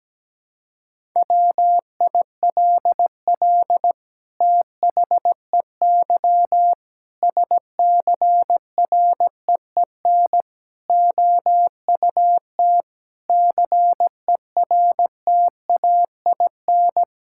Morse code, heard as a keyed tone, about 17 wpm, 700 Hz. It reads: WILL THEY SCREEN OUT CERTAIN
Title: Sentences from Top 1000 Words - Review Part 6 - 17wpm